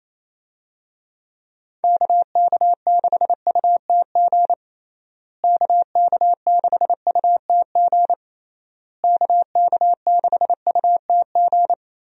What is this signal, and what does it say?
Morse code, heard as a keyed tone, about 28 wpm, 700 Hz. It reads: KK6UTG KK6UTG KK6UTG
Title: USA Call Signs - 6 Characters - Repeated 3x - 28wpm